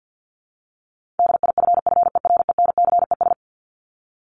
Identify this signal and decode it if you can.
Morse code, heard as a keyed tone, about 50 wpm, 700 Hz. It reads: DIFFERENCES
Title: ES:S Suffix - Top 100 Words with Base Words 50wpm